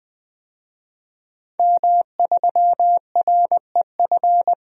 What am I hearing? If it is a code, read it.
Morse code, heard as a keyed tone, about 20 wpm, 700 Hz. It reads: M3REF
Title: DXCC-100 Call Signs - 5 Characters - 20wpm